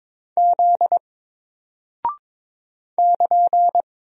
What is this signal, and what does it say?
Morse code, heard as a keyed tone, about 22 wpm, 700 Hz. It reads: Z <KN>
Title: Single Character (Letter:Number:Punctuation:Prosign) Alt 22wpm